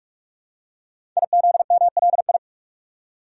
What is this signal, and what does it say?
Morse code, heard as a keyed tone, about 44 wpm, 700 Hz. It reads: IZMLI
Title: Code Groups - 5 Letters - 45wpm